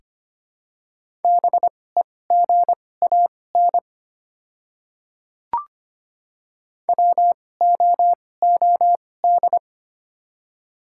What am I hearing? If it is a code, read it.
Morse code, heard as a keyed tone, about 25 wpm, spaced as for 18 wpm, 700 Hz. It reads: BEGAN WOOD
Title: 2x Character Spacing - Top 300 Words - 25wpm